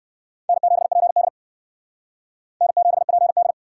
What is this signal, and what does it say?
Morse code, heard as a keyed tone, about 58 wpm, 700 Hz. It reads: N7PD N7PD